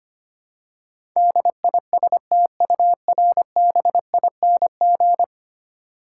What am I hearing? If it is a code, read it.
Morse code, heard as a keyed tone, about 25 wpm, 700 Hz. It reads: DISTURBING
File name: Ten Letter Words - 25wpm